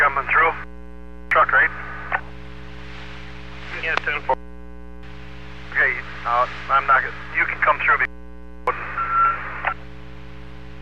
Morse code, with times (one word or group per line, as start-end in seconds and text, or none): none